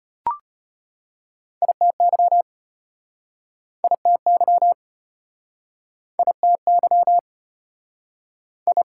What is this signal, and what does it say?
Morse code, heard as a keyed tone, about 32 wpm, 690 Hz. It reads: ITY ITY ITY I